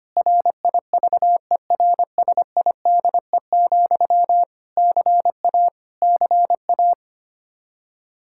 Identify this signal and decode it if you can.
Morse code, heard as a keyed tone, about 25 wpm, 700 Hz. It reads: RIVERSIDE, CA CA
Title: K1USN SST QSO Practice 25wpm